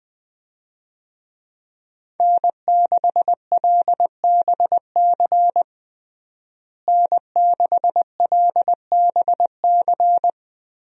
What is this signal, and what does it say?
Morse code, heard as a keyed tone, about 20 wpm, 700 Hz. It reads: N6LBC N6LBC
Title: USA Call Signs - 5 Characters - Repeated 3x - 20wpm